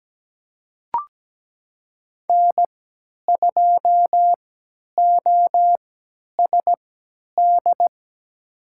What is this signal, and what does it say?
Morse code, heard as a keyed tone, about 17 wpm, spaced as for 10 wpm, 700 Hz. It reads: N2OSD
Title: USA Call Signs - 5 Characters - 3x Character Spacing - 17wpm